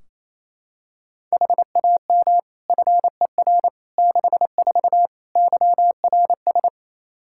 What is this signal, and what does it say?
Morse code, heard as a keyed tone, about 28 wpm, 700 Hz. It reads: HAM FER 64 YRS
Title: Practice QSO - Exchange 3 - More Info - 28wpm